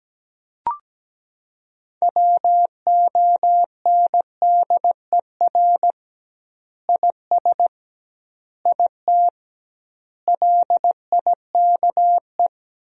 Morse code, heard as a keyed tone, about 17 wpm, 700 Hz. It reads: WONDER IS IT LIKE
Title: 2x Word-Spacing - Sentences from Top 500 Words - 17wpm